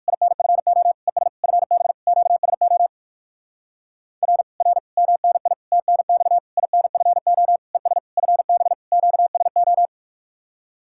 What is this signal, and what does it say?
Morse code, heard as a keyed tone, about 56 wpm, 700 Hz. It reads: INFO ES FB QSO R R MNI TNX INFO ES FB QSO